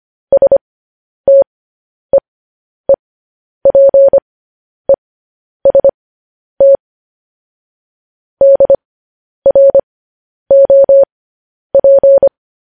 Morse code, heard as a keyed tone, about 25 wpm, spaced as for 10 wpm, 550 Hz. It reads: STEEPEST DROP
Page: news headlines in morse code at 10 wpm